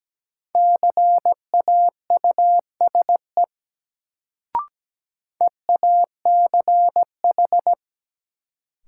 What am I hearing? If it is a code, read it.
Morse code, heard as a keyed tone, about 17 wpm, 700 Hz. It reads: CAUSE EACH
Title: Sentences from Top 200 Words - New Word Drill - 17wpm